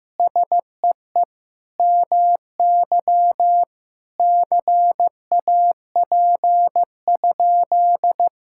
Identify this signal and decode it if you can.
Morse code, heard as a keyed tone, about 16 wpm, 700 Hz. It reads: SEE MY CAP?
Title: Sentences from Top 1600 Words 15wpm